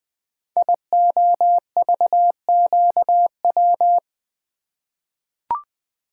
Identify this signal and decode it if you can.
Morse code, heard as a keyed tone, about 20 wpm, 700 Hz. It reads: IOVQW
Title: Code Groups - 5 Letter:Numbers - 20wpm